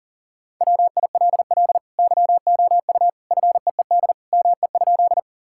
Morse code, heard as a keyed tone, about 37 wpm, 700 Hz. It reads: WILL YOU FEED ME?
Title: Sentences from Top 1600 Words 40wpm